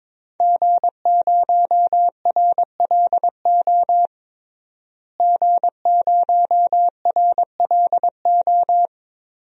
Morse code, heard as a keyed tone, about 22 wpm, 700 Hz. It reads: G0RLO G0RLO
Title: DXCC-100 Call Signs - 5 Characters - Repeated 3x - 22wpm